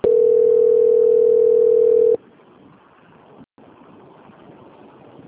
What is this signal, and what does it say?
Ringback tone (North America), 440 + 480 Hz.